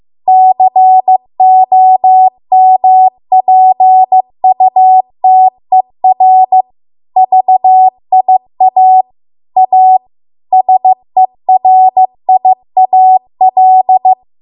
Morse code, heard as a keyed tone, about 15 wpm, 750 Hz. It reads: COMPUTER VIA A SERIAL